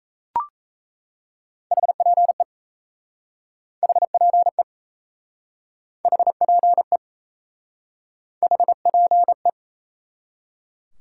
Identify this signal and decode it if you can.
Morse code, heard as a keyed tone, about 38 wpm, 700 Hz. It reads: HPE HPE HPE HPE